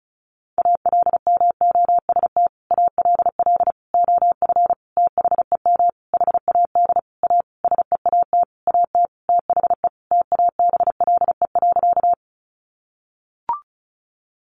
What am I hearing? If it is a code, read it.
Morse code, heard as a keyed tone, about 34 wpm, 700 Hz. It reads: ALMOST ALL OF THEM HAD A SEAT AT THE TABLE.
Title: Sentences from Top 1500 Words 35wpm